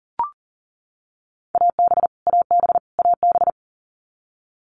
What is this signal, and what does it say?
Morse code, heard as a keyed tone, about 40 wpm, 700 Hz. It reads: AB AB AB
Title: Canadian Provinces and Territories - Repeated 3x - 40wpm